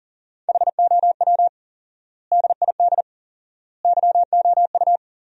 Morse code, heard as a keyed tone, about 37 wpm, 700 Hz. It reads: HOW DID YOU
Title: 4x Word-Spacing - Sentences from Top 100 Words - 40wpm